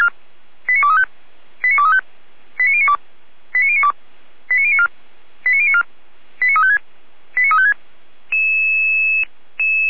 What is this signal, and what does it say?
Five-tone selective call (ZVEI), heard as quick runs of five tones, tones of about 70 ms. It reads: 89235 89235 89092 89092 89094 89094 89356 89356